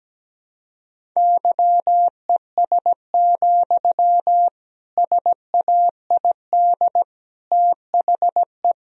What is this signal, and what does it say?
Morse code, heard as a keyed tone, about 17 wpm, 700 Hz. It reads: YES, SAID THE
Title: Sentences from Top 1000 Words - Review Part 4 - 17wpm